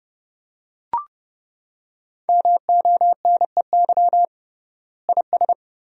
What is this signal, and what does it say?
Morse code, heard as a keyed tone, about 32 wpm, 700 Hz. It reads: MONEY IS
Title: 3x Word-Spacing - Sentences from Top 500 Words - 30wpm